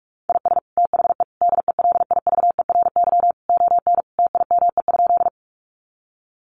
Morse code, heard as a keyed tone, about 44 wpm, 700 Hz. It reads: IS THE DELIVERY ON TIME?
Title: Sentences from Top 1500 Words 45wpm